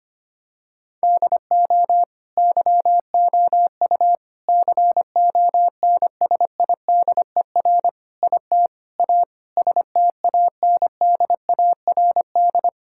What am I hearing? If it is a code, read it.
Morse code, heard as a keyed tone, about 25 wpm, 700 Hz. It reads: DO YOU CONSIDER IT A STANDARD
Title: Sentences from Top 1000 Words - Review Part 6 - 25wpm